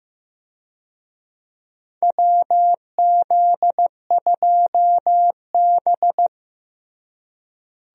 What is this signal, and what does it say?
Morse code, heard as a keyed tone, about 15 wpm, 700 Hz. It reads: WZ2B
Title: USA Call Signs - 4 Characters - Repeated 2x - 15wpm